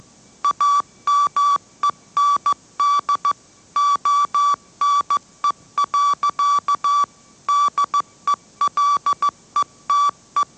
Morse code, heard as a keyed tone, about 15 wpm, 1150 Hz. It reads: AMEND ONE. DELETE
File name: mrx / mrx065 16wpm